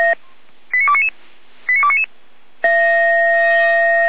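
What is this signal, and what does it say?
Five-tone selective call (ZVEI), heard as quick runs of five tones, tones of about 70 ms. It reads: 89299 89299
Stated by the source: SDR-received ZVEI five-tone selective call 89276, recorded by feuerwehr-orlamuende (fire brigade Orlamünde).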